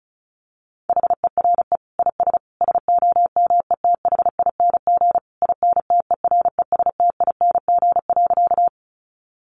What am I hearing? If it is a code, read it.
Morse code, heard as a keyed tone, about 34 wpm, 700 Hz. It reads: HERE IS SOMETHING INTERESTING.
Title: Sentences from Top 1000 Words 35wpm